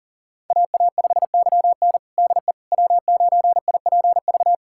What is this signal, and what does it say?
Morse code, heard as a keyed tone, about 40 wpm, 700 Hz. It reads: AA5YN DE W9IPV
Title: USA Callsign DE Callsign - 5 Characters - 40wpm